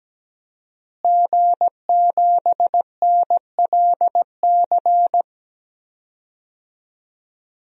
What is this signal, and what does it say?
Morse code, heard as a keyed tone, about 17 wpm, 700 Hz. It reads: G7NLC
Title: DXCC-100 Call Signs - 5 Characters - Repeated 2x - 17wpm